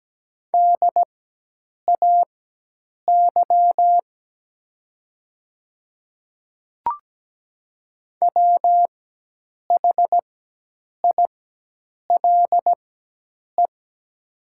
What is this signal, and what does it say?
Morse code, heard as a keyed tone, about 17 wpm, spaced as for 8 wpm, 700 Hz. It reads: DAY WHILE
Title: 4x Character Spacing - Top 300 Words - 17wpm